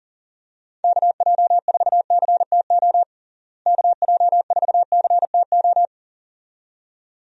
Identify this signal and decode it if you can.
Morse code, heard as a keyed tone, about 40 wpm, 700 Hz. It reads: KJ4CTO KJ4CTO
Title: USA Call Signs - 6 Characters - Repeated 2x - 40wpm